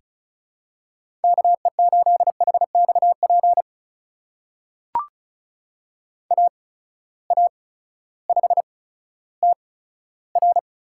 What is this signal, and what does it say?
Morse code, heard as a keyed tone, about 34 wpm, spaced as for 16 wpm, 700 Hz. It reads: KE8HXP AA5TR
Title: USA Call Signs - 8x Character Spacing - 35wpm